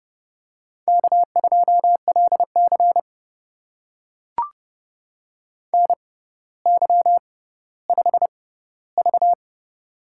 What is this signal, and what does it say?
Morse code, heard as a keyed tone, about 30 wpm, spaced as for 15 wpm, 700 Hz. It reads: K2LC NY5V
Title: USA Call Signs - 4 Characters - 6x Character Spacing - 30wpm